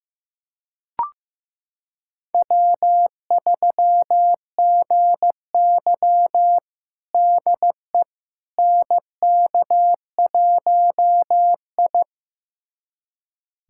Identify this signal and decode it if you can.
Morse code, heard as a keyed tone, about 15 wpm, 700 Hz. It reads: W3GY DE NK1I
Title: USA Callsign DE Callsign - 4 Characters - 15wpm